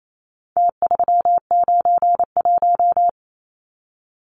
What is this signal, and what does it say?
Morse code, heard as a keyed tone, about 28 wpm, 700 Hz. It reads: T391
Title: Numbers - Contest Style - 4-Digit 28wpm